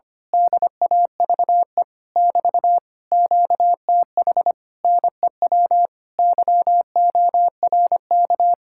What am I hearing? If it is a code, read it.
Morse code, heard as a keyed tone, about 24 wpm, 700 Hz. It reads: DAVE <BT> QTH NEW YORK
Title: Speed-Racing Common QSO Phrases - 25wpm